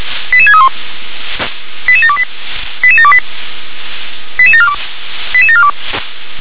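Five-tone selective call (ZVEI), heard as quick runs of five tones, tones of about 70 ms. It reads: 88631 88728 88728 88742 88742
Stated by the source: ZVEI five-tone selective call 88730 received with an SDR.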